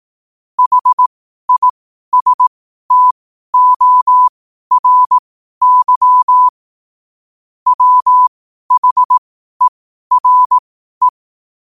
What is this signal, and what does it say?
Morse code, heard as a keyed tone, about 18 wpm, spaced as for 12 wpm, 1000 Hz. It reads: HISTORY WHERE
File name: qotdfiles / 12wpm